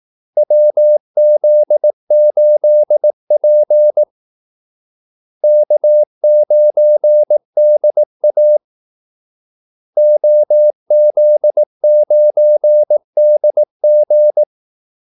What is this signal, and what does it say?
Morse code, heard as a keyed tone, about 18 wpm, 600 Hz. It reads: WZ8P K9DA OZ9DG
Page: Kaldesignaler 18 wpm